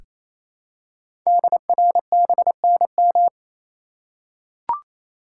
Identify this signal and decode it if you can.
Morse code, heard as a keyed tone, about 28 wpm, 700 Hz. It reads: DRBNM
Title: Code Groups - 5 Letters - 28wpm